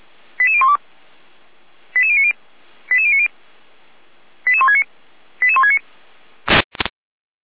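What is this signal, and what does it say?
Five-tone selective call (ZVEI), heard as quick runs of five tones, tones of about 70 ms. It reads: 88012 88090 88090 88169 88169
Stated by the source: Received on an SDR; ZVEI five-tone code 88166.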